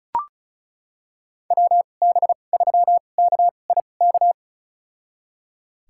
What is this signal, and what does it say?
Morse code, heard as a keyed tone, about 34 wpm, spaced as for 25 wpm, 700 Hz. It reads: WB3KIK
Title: USA Call Signs - 6 Characters - 2x Character Spacing - 35wpm